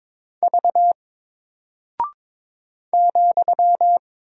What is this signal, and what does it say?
Morse code, heard as a keyed tone, about 22 wpm, 700 Hz. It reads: V ,
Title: Single Character (Letter:Number:Punctuation:Prosign) 22wpm